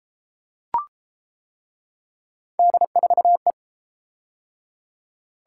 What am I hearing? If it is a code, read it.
Morse code, heard as a keyed tone, about 30 wpm, 700 Hz. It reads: D4E